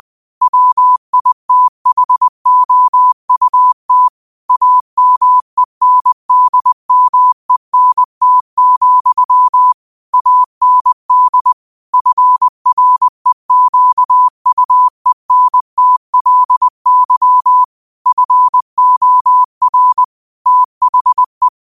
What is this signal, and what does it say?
Morse code, heard as a keyed tone, about 20 wpm, 1000 Hz. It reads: WITHOUT AMENDMENT, AND FREQUENTLY FOR THE